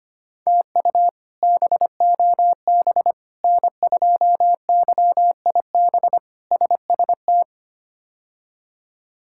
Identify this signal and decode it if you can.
Morse code, heard as a keyed tone, about 25 wpm, 700 Hz. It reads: TU BOB N2YIB SST
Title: K1USN SST QSO Practice - 25wpm